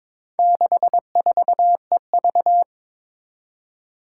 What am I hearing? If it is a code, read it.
Morse code, heard as a keyed tone, about 22 wpm, 700 Hz. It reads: 64EV
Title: Four Challenging Letter-Number Combinations 22wpm